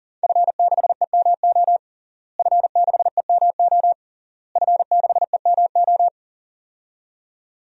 Morse code, heard as a keyed tone, about 40 wpm, 700 Hz. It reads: F6EMO F6EMO F6EMO
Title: DXCC-100 Call Signs - 5 Characters - Repeated 3x - 40wpm